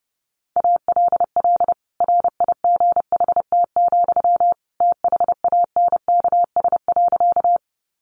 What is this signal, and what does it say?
Morse code, heard as a keyed tone, about 30 wpm, 700 Hz. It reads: ALL RIGHT, THANKS.
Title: Sentences from Top 600 Words 30wpm